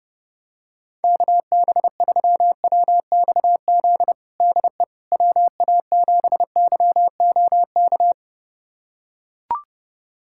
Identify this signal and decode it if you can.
Morse code, heard as a keyed tone, about 32 wpm, 700 Hz. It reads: KB3WXZ DE WA7YOK
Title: USA Callsign DE Callsign - 6 Characters - 30wpm